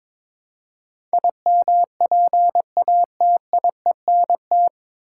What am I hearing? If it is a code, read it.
Morse code, heard as a keyed tone, about 22 wpm, 700 Hz. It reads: IMPATIENT
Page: Morse Code Ninja practice 22wpm